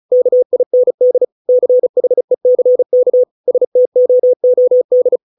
Morse code, heard as a keyed tone, about 34 wpm, 500 Hz. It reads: KIND CHECK STOOD